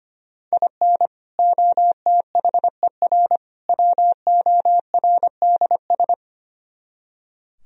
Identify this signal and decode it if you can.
Morse code, heard as a keyed tone, about 31 wpm, 700 Hz. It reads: IN OTHER WORDS
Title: Speed-Racing - Sets of 3 Words 25wpm